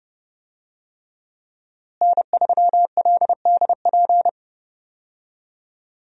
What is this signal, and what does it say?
Morse code, heard as a keyed tone, about 30 wpm, 700 Hz. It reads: N3LDP
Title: USA Call Signs - 5 Characters - Repeated 3x - 30wpm